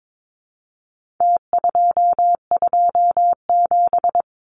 Morse code, heard as a keyed tone, about 22 wpm, 700 Hz. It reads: T227
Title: Numbers - Contest Style - 4-Digit 22wpm